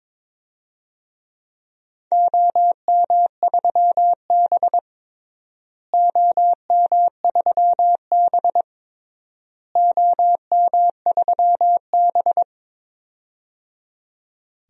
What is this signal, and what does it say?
Morse code, heard as a keyed tone, about 22 wpm, 700 Hz. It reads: OM3B OM3B OM3B
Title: DXCC-100 Call Signs - 4 Characters - Repeated 3x - 22wpm